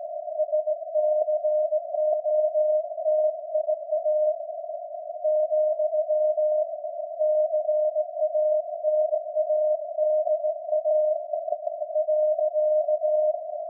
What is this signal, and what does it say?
Morse code, heard as a keyed tone, about 18 wpm, 610 Hz. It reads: SCOTIA , CANADA .
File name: iz7slz / VLF / SAQ jun 2019